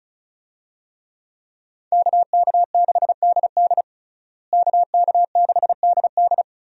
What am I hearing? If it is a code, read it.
Morse code, heard as a keyed tone, about 34 wpm, 700 Hz. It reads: KK6DD KK6DD
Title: USA Call Signs - 5 Characters - Repeated 2x - 35wpm